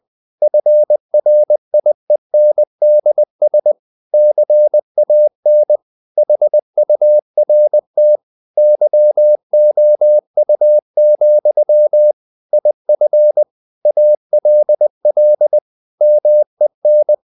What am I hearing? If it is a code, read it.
Morse code, heard as a keyed tone, about 20 wpm, 600 Hz. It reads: FRIENDS CAN HURT YOU, IF ALL MEN
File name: CW / Practice / If20